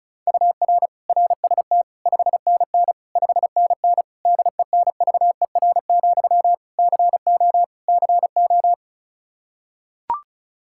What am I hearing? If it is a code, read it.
Morse code, heard as a keyed tone, about 34 wpm, 700 Hz. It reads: UR RST 5NN 5NN DENVER, CO CO
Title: Practice QSO - Exchange 1 - Beginning - 35wpm